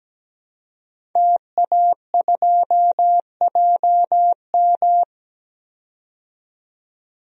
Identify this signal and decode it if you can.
Morse code, heard as a keyed tone, about 17 wpm, 700 Hz. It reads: TA2JM
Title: DXCC-100 Call Signs - 5 Characters - 17wpm